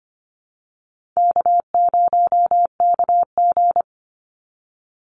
Morse code, heard as a keyed tone, about 25 wpm, 700 Hz. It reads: K0KG
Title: Easy Call Signs Repeated 2x - 25wpm